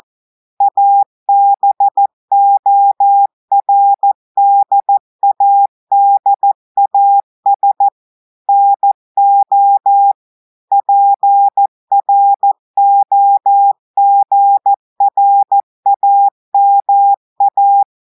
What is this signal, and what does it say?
Morse code, heard as a keyed tone, about 14 wpm, 800 Hz. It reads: ABORDADAS NO PROGRAMA
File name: audios / portugais / portugais texte en clair 2 vitesse 14 mots minutes